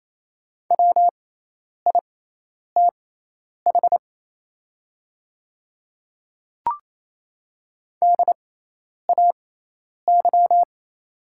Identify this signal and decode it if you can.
Morse code, heard as a keyed tone, about 28 wpm, spaced as for 10 wpm, 700 Hz. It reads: WITH DAY
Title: 6x Character Spacing - Top 100 Words - 28wpm